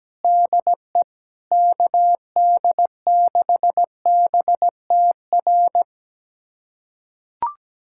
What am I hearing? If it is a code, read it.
Morse code, heard as a keyed tone, about 17 wpm, 700 Hz. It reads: DE KD6BTR
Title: USA Callsign DE Callsign 17wpm